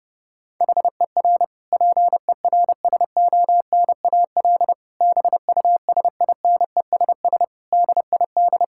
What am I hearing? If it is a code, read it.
Morse code, heard as a keyed tone, about 30 wpm, 700 Hz. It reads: HER PERSONAL BUSINESS DID
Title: Sentences from Top 1000 Words - Review Part 6 - 30wpm